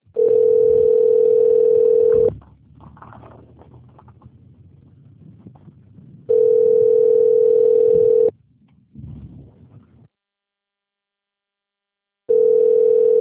Ringback tone (North America), 440 + 480 Hz.
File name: rg-1600-unknown- - - .15993